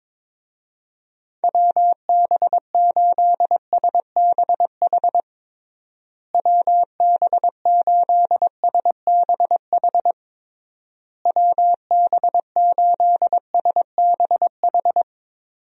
Morse code, heard as a keyed tone, about 22 wpm, 700 Hz. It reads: WB8SBH WB8SBH WB8SBH
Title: USA Call Signs - 6 Characters - Repeated 3x - 22wpm